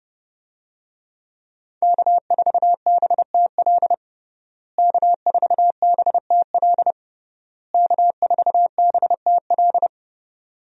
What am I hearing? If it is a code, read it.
Morse code, heard as a keyed tone, about 32 wpm, 700 Hz. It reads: K4BTL K4BTL K4BTL
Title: USA Call Signs - 5 Characters - Repeated 3x - 30wpm